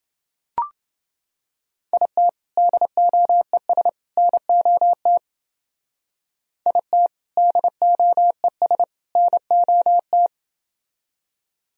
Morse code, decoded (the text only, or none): IT DOES NOT IT DOES NOT